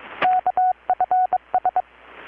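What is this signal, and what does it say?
Morse code, heard as a keyed tone, about 22 wpm, 720 Hz, with radio noise. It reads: KFS
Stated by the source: Morse from KPH's Night of Nights XX maritime radiotelegraphy event, received in Chiapas, Mexico about 3,800 km away on a Kenwood R-600 with a grounded 10 m long-wire antenna.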